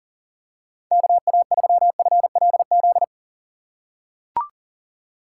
Morse code, heard as a keyed tone, about 37 wpm, 700 Hz. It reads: KA3FLZ
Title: USA Call Signs - 6 Characters - 4x Character Spacing - 40wpm